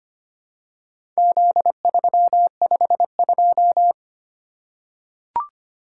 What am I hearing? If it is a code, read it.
Morse code, heard as a keyed tone, about 25 wpm, 700 Hz. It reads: Z352
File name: practice / 25wpm / Four Letter-Number Combinations - 25wpm